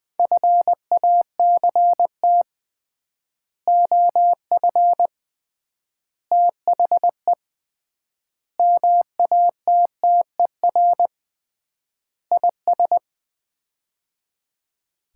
Morse code, heard as a keyed tone, about 20 wpm, 700 Hz. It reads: FACT OF THE MATTER IS
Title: 3x Word-Spacing - Sets of 5 Words - 20wpm